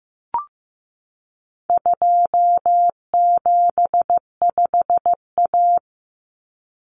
Morse code, heard as a keyed tone, about 15 wpm, 700 Hz. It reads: 275A
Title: Common Cut Numbers - 4-Digit 15wpm